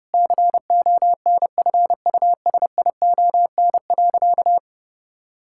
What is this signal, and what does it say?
Morse code, heard as a keyed tone, about 30 wpm, 700 Hz. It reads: CONFUSION.
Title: Sentences from Top 1600 Words 30wpm